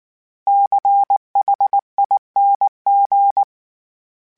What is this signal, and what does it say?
Morse code, heard as a keyed tone, about 19 wpm, 800 Hz. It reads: CHING
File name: Special INGs / Special INGs-mp3s / Sngl Ing @18